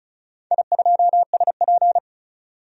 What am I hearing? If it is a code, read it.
Morse code, heard as a keyed tone, about 36 wpm, 700 Hz. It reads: I2SP
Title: DXCC-100 Call Signs - 4 Characters - 35wpm